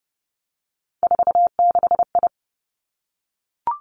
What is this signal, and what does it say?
Morse code, heard as a keyed tone, about 30 wpm, 700 Hz. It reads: 46I
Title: Three Challenging Letter-Number Combinations 30wpm